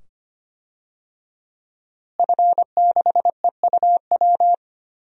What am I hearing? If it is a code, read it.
Morse code, heard as a keyed tone, about 25 wpm, 700 Hz. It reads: F6EUW
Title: DXCC-100 Call Signs - 5 Characters - Repeated 2x - 25wpm